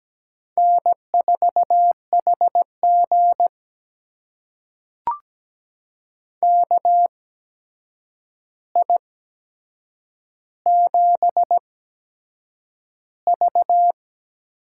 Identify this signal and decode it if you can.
Morse code, heard as a keyed tone, about 17 wpm, spaced as for 7 wpm, 700 Hz. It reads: N4HG KI7V
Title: USA Call Signs - 8x Character Spacing - 17wpm